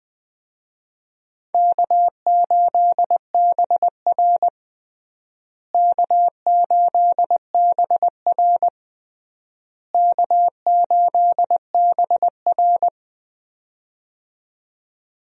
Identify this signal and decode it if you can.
Morse code, heard as a keyed tone, about 21 wpm, 700 Hz. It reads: K8BR K8BR K8BR
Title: USA Call Signs - 4 Characters - Repeated 3x - 20wpm